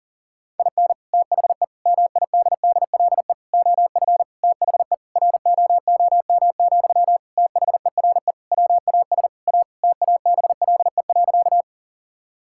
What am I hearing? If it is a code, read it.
Morse code, heard as a keyed tone, about 40 wpm, 700 Hz. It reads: IN THE MIDDLE OF THE ROOM, THERE WAS A TABLE.
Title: Sentences from Top 1000 Words - Review Part 6 - 40wpm